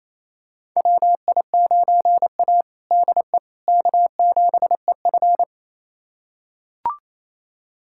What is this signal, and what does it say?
Morse code, heard as a keyed tone, about 28 wpm, 700 Hz. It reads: WI9A DE K7EF
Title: USA Callsign DE Callsign 28wpm